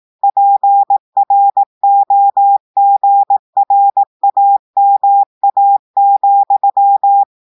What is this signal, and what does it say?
Morse code, heard as a keyed tone, about 18 wpm, 800 Hz. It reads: PROGRAMA,
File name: audios / portugais / portugais texte en clair 2 vitesse 18 mots minutes